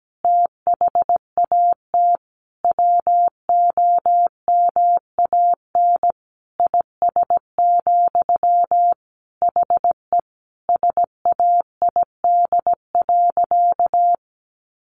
Morse code, heard as a keyed tone, about 17 wpm, 700 Hz. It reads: THAT WOMAN IS, HE SAID.
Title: Sentences from Top 1200 Words 17wpm